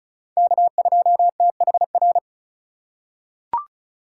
Morse code, heard as a keyed tone, about 36 wpm, 700 Hz. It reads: K2THR
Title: USA Call Signs - 2x Character Spacing - 35wpm